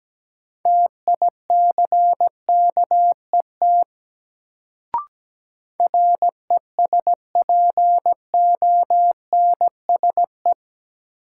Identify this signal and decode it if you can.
Morse code, heard as a keyed tone, about 17 wpm, 700 Hz. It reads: TICKET RESPONSE
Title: Sentences from Top 1400 Words - New Word Drill - 17wpm